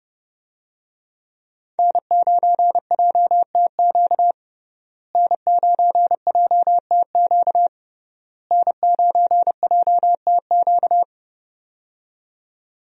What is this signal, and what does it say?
Morse code, heard as a keyed tone, about 30 wpm, 700 Hz. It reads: N9JTQ N9JTQ N9JTQ
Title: USA Call Signs - 5 Characters - Repeated 3x - 30wpm